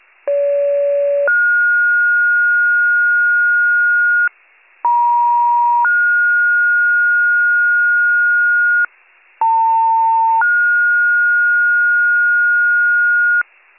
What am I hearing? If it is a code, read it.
Two-tone page, heard as a one-second tone, then a three-second tone, about 570 then 1430 Hz, about 950 then 1430 Hz, about 900 then 1430 Hz. It is heard from 0.3 to 13.4 s.